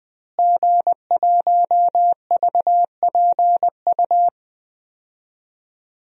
Morse code, heard as a keyed tone, about 20 wpm, 700 Hz. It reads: G1VPU
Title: DXCC-100 Call Signs - 5 Characters - Repeated 3x - 20wpm